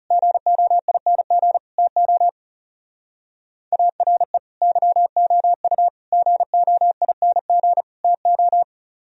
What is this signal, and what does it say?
Morse code, heard as a keyed tone, about 37 wpm, 700 Hz. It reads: GOING TO ARE YOU GOING TO
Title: Speed-Racing - Sets of 4 Words 30wpm